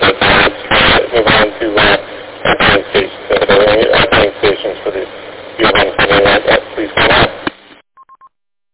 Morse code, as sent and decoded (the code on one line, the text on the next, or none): none